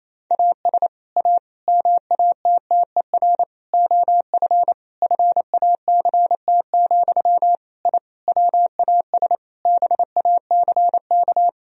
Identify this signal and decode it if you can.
Morse code, heard as a keyed tone, about 28 wpm, 700 Hz. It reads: AS A MATTER OF FACT, I WAS BACK
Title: Sentences from Top 1000 Words - Review Part 4 - 28wpm